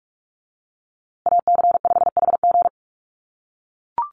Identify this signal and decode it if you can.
Morse code, heard as a keyed tone, about 44 wpm, 700 Hz. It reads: AC5HG
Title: USA Call Signs Repeated 2x 45wpm